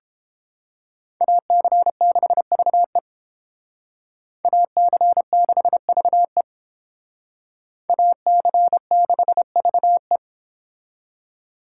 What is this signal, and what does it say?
Morse code, heard as a keyed tone, about 30 wpm, 700 Hz. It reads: AC6VE AC6VE AC6VE